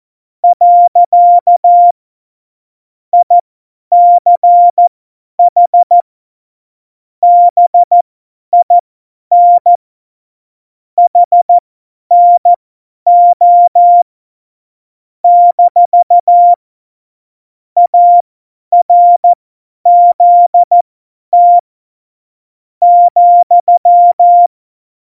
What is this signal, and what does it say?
Morse code, heard as a keyed tone, about 14 wpm, spaced as for 10 wpm, 700 Hz. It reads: . ICH BIN HNO - ARZT ,